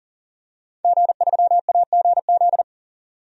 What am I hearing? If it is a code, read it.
Morse code, heard as a keyed tone, about 40 wpm, 700 Hz. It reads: G3AGZ
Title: DXCC-100 Call Signs - 5 Characters - 40wpm